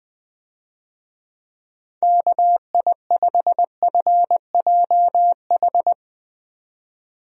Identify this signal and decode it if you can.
Morse code, heard as a keyed tone, about 20 wpm, 700 Hz. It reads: KI5FJH
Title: USA Call Signs - 6 Characters - 20wpm